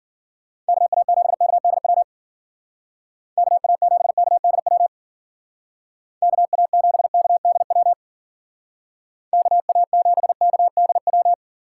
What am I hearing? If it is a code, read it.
Morse code, heard as a keyed tone, about 49 wpm, 700 Hz. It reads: KA7KDW KA7KDW KA7KDW KA7KDW
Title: Speed Racing - USA 2x3 Call Signs 40wpm